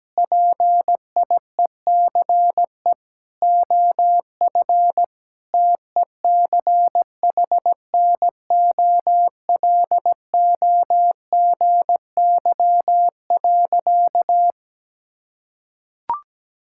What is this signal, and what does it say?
Morse code, heard as a keyed tone, about 17 wpm, 700 Hz. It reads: PIECE OF TECHNOLOGY.
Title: Sentences from Top 1000 Words - Review Part 5 - 17wpm